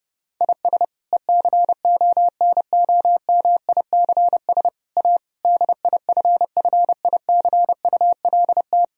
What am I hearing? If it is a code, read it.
Morse code, heard as a keyed tone, about 30 wpm, 700 Hz. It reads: IS ECONOMICS A DIFFICULT
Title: Sentences from Top 1000 Words - Review Part 3 - 30wpm